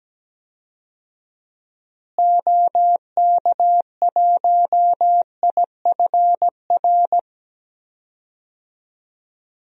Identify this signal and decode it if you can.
Morse code, heard as a keyed tone, about 17 wpm, 700 Hz. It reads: OK1IFR
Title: DXCC-100 Call Signs - 6 Characters - Repeated 3x - 17wpm